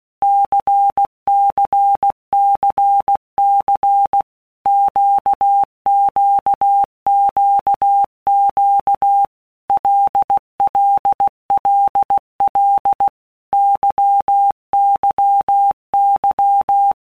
Morse code, heard as a keyed tone, about 16 wpm, 800 Hz. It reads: CCCC QQQQ LLLL YYY
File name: Basic Letter Groups By Blocks / Groups@15 / LFCYQ@15